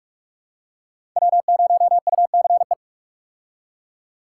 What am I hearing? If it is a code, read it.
Morse code, heard as a keyed tone, about 47 wpm, 700 Hz. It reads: W0UCE
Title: K1USN SST QSO Practice - 45wpm